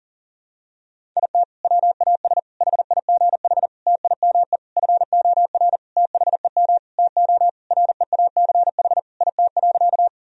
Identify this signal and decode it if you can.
Morse code, heard as a keyed tone, about 40 wpm, 700 Hz. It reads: IT WAS HIGH TIME FOR THEM TO REACH IT.
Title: Sentences from Top 1000 Words - Review Part 2 - 40wpm